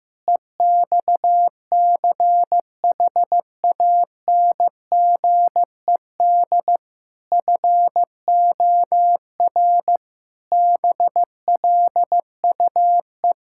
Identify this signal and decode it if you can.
Morse code, heard as a keyed tone, about 16 wpm, 700 Hz. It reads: EXCHANGED FOR BLUE
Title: Pangram Challenge v2 - 15wpm